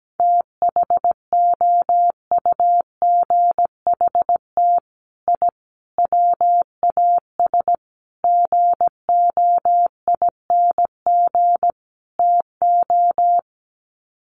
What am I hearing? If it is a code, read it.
Morse code, heard as a keyed tone, about 17 wpm, 700 Hz. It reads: THOUGHT I WAS GOING TO
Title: Sets of 5 Words Encore2 17wpm